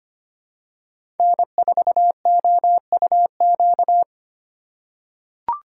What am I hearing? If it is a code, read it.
Morse code, heard as a keyed tone, about 25 wpm, 700 Hz. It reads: N4OUQ
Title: USA Call Signs - 4x Character Spacing - 25wpm